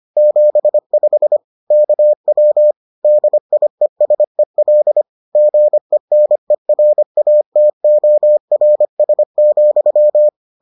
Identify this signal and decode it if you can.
Morse code, heard as a keyed tone, about 25 wpm, 600 Hz. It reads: 75 KW DIESEL GENERATORS,